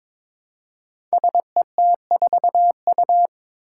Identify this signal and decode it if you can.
Morse code, heard as a keyed tone, about 22 wpm, 700 Hz. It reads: SET4U
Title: Five Challenging Letter-Number Combinations 22wpm